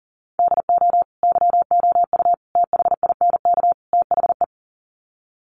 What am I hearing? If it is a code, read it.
Morse code, heard as a keyed tone, about 40 wpm, 700 Hz. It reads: DO YOU THINK THE
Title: Sets of 4 Words 40wpm